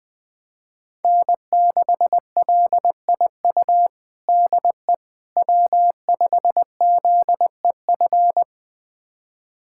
Morse code, heard as a keyed tone, about 21 wpm, 700 Hz. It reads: N6LIU DE W5ZEF
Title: USA Callsign DE Callsign - 5 Characters - 20wpm